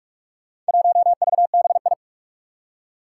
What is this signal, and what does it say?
Morse code, heard as a keyed tone, about 44 wpm, 700 Hz. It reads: 1VBI
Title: Four Challenging Letter-Number Combinations 45wpm